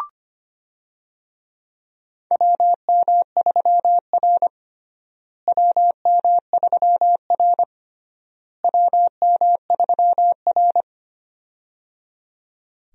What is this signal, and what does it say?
Morse code, heard as a keyed tone, about 25 wpm, 700 Hz. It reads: WM3R WM3R WM3R
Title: USA Call Signs - 4 Characters - Repeated 3x - 25wpm